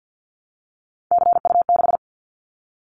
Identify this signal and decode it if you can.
Morse code, heard as a keyed tone, about 50 wpm, 700 Hz. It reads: CU6
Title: Callsign Prefixes for Top 50 DXCC - 50wpm